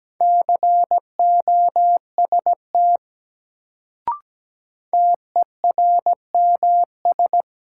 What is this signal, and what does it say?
Morse code, heard as a keyed tone, about 17 wpm, 700 Hz. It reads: COST TERMS
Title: Sets of 3 Words - New Word Drill - 17wpm